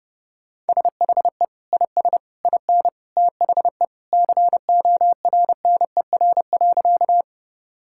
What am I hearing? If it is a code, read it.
Morse code, heard as a keyed tone, about 30 wpm, 700 Hz. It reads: SHE IS IN THE CORNER.
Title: Sentences from Top 1600 Words 30wpm